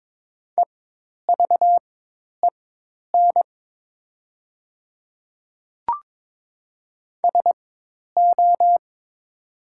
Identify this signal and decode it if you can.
Morse code, heard as a keyed tone, about 22 wpm, spaced as for 10 wpm, 700 Hz. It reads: EVEN SO
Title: 4x Character Spacing - Top 200 Words - 22wpm